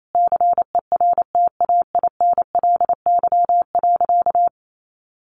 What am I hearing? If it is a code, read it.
Morse code, heard as a keyed tone, about 28 wpm, 700 Hz. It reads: CERTAINLY.
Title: Sentences from Top 900 Words 28wpm